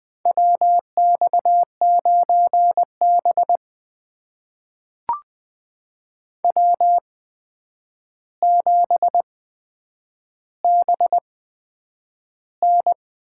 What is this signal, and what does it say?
Morse code, heard as a keyed tone, about 20 wpm, spaced as for 9 wpm, 700 Hz. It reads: WX9B W7BN
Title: USA Call Signs - 4 Characters - 8x Character Spacing - 20wpm